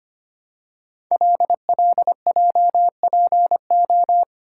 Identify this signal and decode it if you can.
Morse code, heard as a keyed tone, about 25 wpm, 700 Hz. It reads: LLJPO
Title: Code Groups - 5 Letters - 25wpm